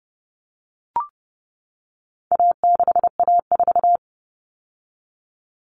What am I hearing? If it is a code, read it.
Morse code, heard as a keyed tone, about 29 wpm, 700 Hz. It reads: A6A4
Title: All Cut Numbers - 4-Digit 30wpm